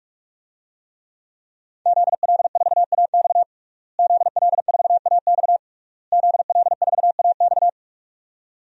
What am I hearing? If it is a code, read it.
Morse code, heard as a keyed tone, about 41 wpm, 700 Hz. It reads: ZL4AX ZL4AX ZL4AX
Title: DXCC-100 Call Signs - 5 Characters - Repeated 3x - 45wpm